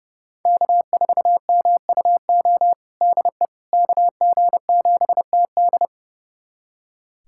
Morse code, heard as a keyed tone, about 32 wpm, 700 Hz. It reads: K4MUO DE KG7TD
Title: USA Callsign DE Callsign - 5 Characters - 30wpm